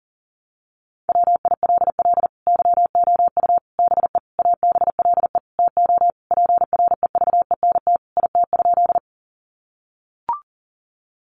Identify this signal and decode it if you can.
Morse code, heard as a keyed tone, about 40 wpm, 700 Hz. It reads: WILL YOU BE ABLE TO PREVENT IT?
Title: Sentences from Top 1100 Words 40wpm